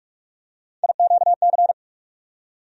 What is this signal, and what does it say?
Morse code, heard as a keyed tone, about 43 wpm, 700 Hz. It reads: IQC
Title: USA Call Sign Postfixes - 3 Letters 45wpm